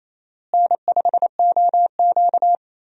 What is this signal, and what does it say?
Morse code, heard as a keyed tone, about 28 wpm, 700 Hz. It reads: N5OQ
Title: USA Call Signs - 4 Characters - Repeated 2x - 28wpm